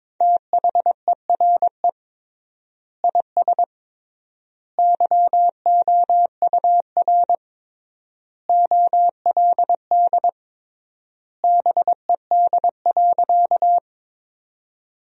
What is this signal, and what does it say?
Morse code, heard as a keyed tone, about 22 wpm, 700 Hz. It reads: THERE IS YOUR OLD BED.
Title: 3x Word-Spacing - Sentences from Top 500 Words - 22wpm